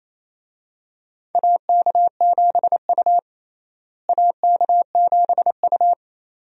Morse code, heard as a keyed tone, about 28 wpm, 700 Hz. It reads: AK7U AK7U
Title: USA Call Signs - 4 Characters - Repeated 2x - 28wpm